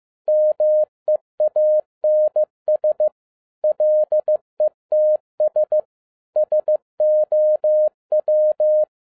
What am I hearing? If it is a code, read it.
Morse code, heard as a keyed tone, about 15 wpm, 610 Hz. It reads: MEANS LETS SOW
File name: 2012-01-mp3 / n13-A NEWLY DESIGNED QSL CARD-15wpm